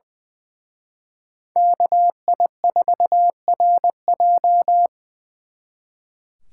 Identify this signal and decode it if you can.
Morse code, heard as a keyed tone, about 23 wpm, 700 Hz. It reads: KI4RJ
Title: Speed Racing - USA 2x2 Call Signs 20wpm